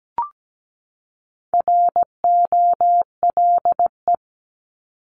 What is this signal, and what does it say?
Morse code, heard as a keyed tone, about 17 wpm, 700 Hz. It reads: ROLE
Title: Top 751-1000 Words 17wpm